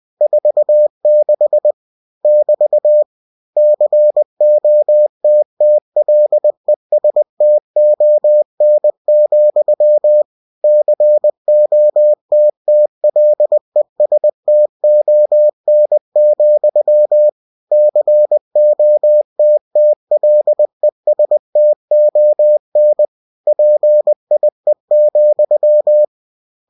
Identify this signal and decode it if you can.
Morse code, heard as a keyed tone, about 20 wpm, 600 Hz. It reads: 46 <BT> COTTLESTON, COTTLESTON, COTTLESTON PIE,
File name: cw-books / winnie-the-pooh / winnie-20wpm-page-0046